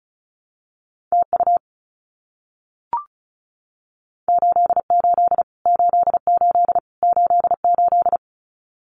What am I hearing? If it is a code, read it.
Morse code, heard as a keyed tone, about 34 wpm, 700 Hz. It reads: TU 88 88 88
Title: Abbreviations Repeated 3x - 35wpm